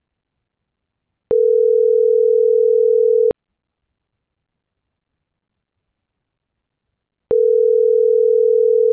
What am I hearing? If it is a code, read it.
Ringback tone (North America), 440 + 480 Hz.
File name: CENTRAL GENERAL / out daniel.valazco